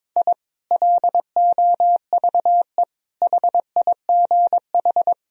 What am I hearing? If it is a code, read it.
Morse code, heard as a keyed tone, about 21 wpm, 700 Hz. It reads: I LOVE HIGH
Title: Sentences from Top 1000 Words - Review Part 4 - 22wpm